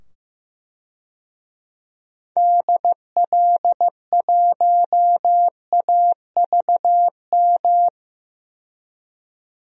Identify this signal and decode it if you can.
Morse code, heard as a keyed tone, about 16 wpm, 700 Hz. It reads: DL1AVM
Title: DXCC-100 Call Signs - 6 Characters - 15wpm